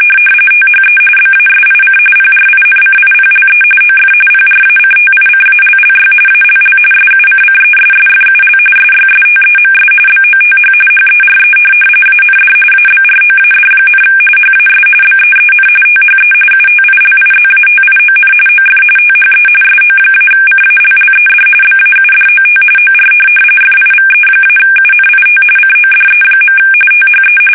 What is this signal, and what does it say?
RTTY, about 45.45 baud. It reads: THE BASIC PRINCIPLE OF PRACTICALLY ALL SUCCESSFUL SYSTEMS BOTH HERE AND ABROAD, HAS BEEN THE UTILIZATION OF THE SO-CALLED BAUDOT OR FIVE-UNIT CODE. THE
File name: fsk850